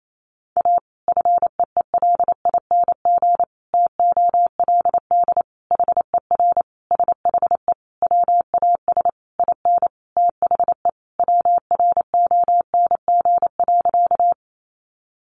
Morse code, heard as a keyed tone, about 28 wpm, 700 Hz. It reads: A FEELING TOLD HER SHE WAS IN THE WRONG.
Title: Sentences from Top 1200 Words 28wpm